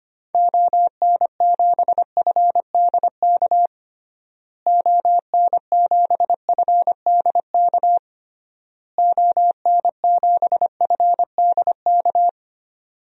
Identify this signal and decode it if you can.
Morse code, heard as a keyed tone, about 25 wpm, 700 Hz. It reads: ON7FDK ON7FDK ON7FDK
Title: DXCC-100 Call Signs - 6 Characters - Repeated 3x - 25wpm